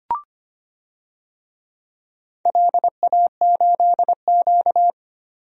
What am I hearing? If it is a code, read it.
Morse code, heard as a keyed tone, about 25 wpm, 700 Hz. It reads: LA8Q
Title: DXCC-100 Call Signs - 4 Characters - 25wpm